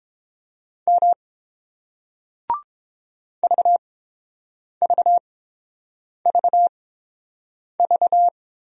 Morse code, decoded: M VVVV

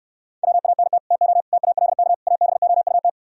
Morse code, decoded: WANT TO TALK ABOUT

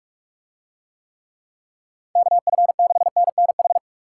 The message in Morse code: KF6NNH

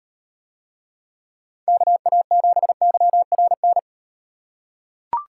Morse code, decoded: KA7YRN

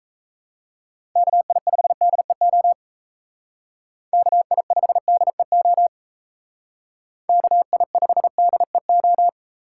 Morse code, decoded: KI5DEO KI5DEO KI5DEO

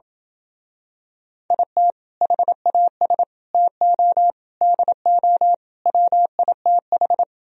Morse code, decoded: IT HAS TO DO WITH

